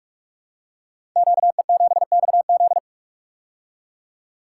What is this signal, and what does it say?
Morse code, heard as a keyed tone, about 44 wpm, 700 Hz. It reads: QE7XZ
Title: Code Groups - 5 Letter:Numbers - 45wpm